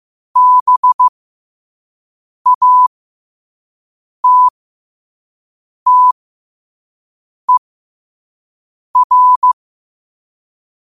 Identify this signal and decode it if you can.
Morse code, heard as a keyed tone, about 15 wpm, spaced as for 5 wpm, 1000 Hz. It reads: BATTER
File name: qotdfiles / 5wpm